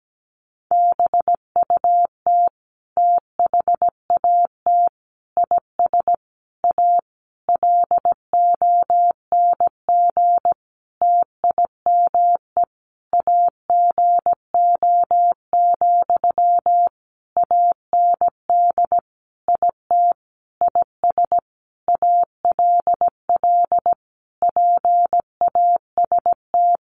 Morse code, heard as a keyed tone, about 17 wpm, 700 Hz. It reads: BUT THAT IS A LONG TIME AGO, AND IT IS ALL PAST
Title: Sentences from Top 600 Words 17wpm